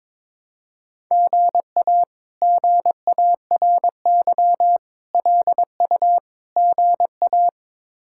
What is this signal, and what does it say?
Morse code, heard as a keyed tone, about 22 wpm, 700 Hz. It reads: GA GARY LU GA
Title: K1USN SST QSO Practice - 22wpm